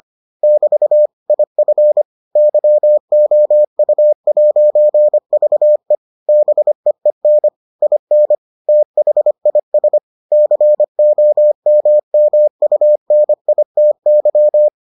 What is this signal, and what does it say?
Morse code, heard as a keyed tone, about 25 wpm, 600 Hz. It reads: <BT> IF YOU'VE BEEN IN THIS COMMUNITY